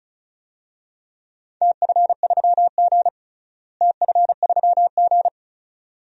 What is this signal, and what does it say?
Morse code, heard as a keyed tone, about 34 wpm, 700 Hz. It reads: TF3G TF3G
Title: DXCC-100 Call Signs - 4 Characters - Repeated 2x - 35wpm